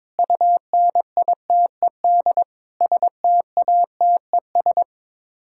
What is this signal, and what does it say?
Morse code, heard as a keyed tone, about 22 wpm, 700 Hz. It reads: UNITED STATES
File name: practice / 22wpm / Sets of 6 Words - 22wpm